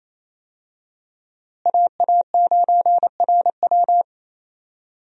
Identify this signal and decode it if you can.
Morse code, heard as a keyed tone, about 28 wpm, 700 Hz. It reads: AA9RW
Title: USA Call Signs - 5 Characters - Repeated 3x - 28wpm